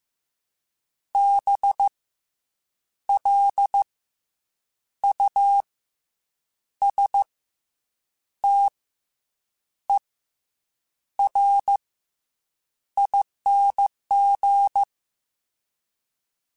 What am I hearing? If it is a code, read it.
Morse code, heard as a keyed tone, about 15 wpm, spaced as for 7 wpm, 800 Hz. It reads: BLUSTERING
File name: Special INGs-mp3s / Special INGs-mp3s / Fast ing @13